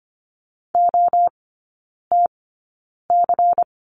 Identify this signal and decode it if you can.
Morse code, heard as a keyed tone, about 25 wpm, spaced as for 9 wpm, 700 Hz. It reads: OTC
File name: course / 25:10wpm / 093 Lesson 16e - Letter c - Three Characters 25wpm